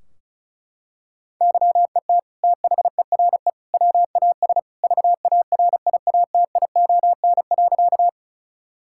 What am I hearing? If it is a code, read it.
Morse code, heard as a keyed tone, about 36 wpm, 700 Hz. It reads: YET THERE WAS VARIATION.
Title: Sentences from Top 1600 Words 35wpm